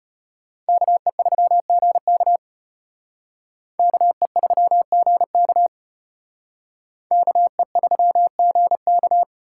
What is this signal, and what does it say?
Morse code, heard as a keyed tone, about 32 wpm, 700 Hz. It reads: KE3GK KE3GK KE3GK